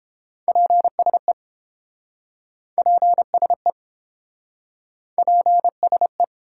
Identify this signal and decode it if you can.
Morse code, heard as a keyed tone, about 29 wpm, 700 Hz. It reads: PSE PSE PSE